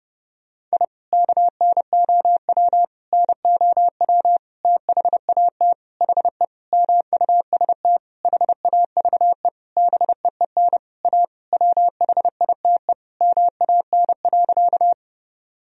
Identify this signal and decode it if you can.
Morse code, heard as a keyed tone, about 30 wpm, 700 Hz. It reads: I KNOW NOW THAT HE MUST HAVE BEEN A WHITE MAN.
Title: Sentences from Top 1000 Words - Review Part 1 - 30wpm